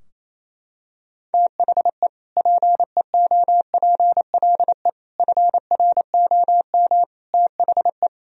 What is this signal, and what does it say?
Morse code, heard as a keyed tone, about 28 wpm, 700 Hz. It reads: THE PEOPLE FROM THE
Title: Sentences from Top 1000 Words - Review Part 3 - 28wpm